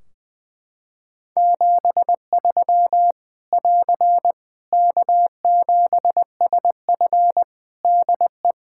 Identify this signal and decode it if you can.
Morse code, heard as a keyed tone, about 20 wpm, 700 Hz. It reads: 73 <AR> K7SF DE
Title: Practice QSO - Exchange 4 - Ending - 20wpm